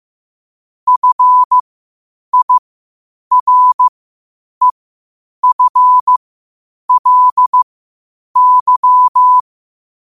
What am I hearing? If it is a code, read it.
Morse code, heard as a keyed tone, about 15 wpm, spaced as for 8 wpm, 1000 Hz. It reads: FIREFLY
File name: qotdfiles / 8wpm